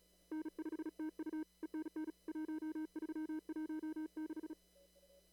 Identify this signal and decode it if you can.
Morse code, heard as a keyed tone, about 36 wpm, 310 Hz. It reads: N5TU ENN 1316
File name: CW / kt5j wpx cw-op-k5pi